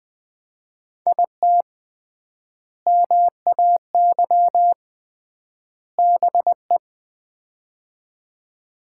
Morse code, heard as a keyed tone, about 21 wpm, 700 Hz. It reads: IT MAY BE
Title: 3x Word-Spacing - Sets of 3 Words - 20wpm